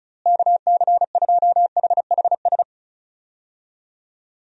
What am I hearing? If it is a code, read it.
Morse code, heard as a keyed tone, about 36 wpm, 700 Hz. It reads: KC2HHS
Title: USA Call Signs - 6 Characters - Repeated 2x - 35wpm